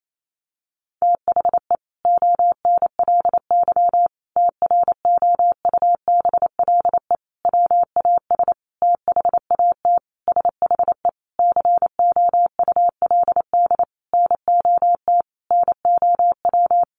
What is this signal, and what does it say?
Morse code, heard as a keyed tone, about 28 wpm, 700 Hz. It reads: THE ONLY TROUBLE WAS THAT SHE COULD NOT NOW
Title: Sentences from Top 1200 Words 28wpm